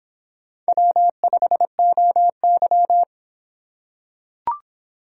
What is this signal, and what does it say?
Morse code, heard as a keyed tone, about 25 wpm, 700 Hz. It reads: W5OY